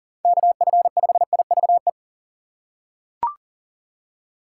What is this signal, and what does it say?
Morse code, heard as a keyed tone, about 37 wpm, 700 Hz. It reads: KF5IVE
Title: USA Call Signs - 6 Characters - 3x Character Spacing - 40wpm